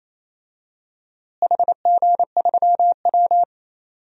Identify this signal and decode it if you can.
Morse code, heard as a keyed tone, about 28 wpm, 700 Hz. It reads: HG3W
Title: DXCC-100 Call Signs - 4 Characters - Repeated 3x - 28wpm